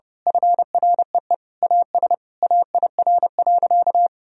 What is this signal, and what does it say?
Morse code, heard as a keyed tone, about 29 wpm, 700 Hz. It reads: FREE AS AIR.
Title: Sentences from Top 1000 Words - Review Part 1 - 30wpm